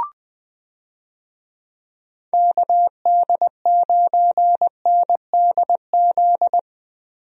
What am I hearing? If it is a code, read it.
Morse code, heard as a keyed tone, about 20 wpm, 700 Hz. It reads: KD9NDZ
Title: USA Call Signs - 6 Characters - Repeated 3x - 20wpm